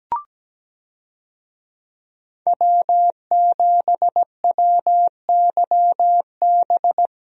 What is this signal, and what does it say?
Morse code, heard as a keyed tone, about 17 wpm, 700 Hz. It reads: W7WYB
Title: USA Call Signs - 5 Characters - 17wpm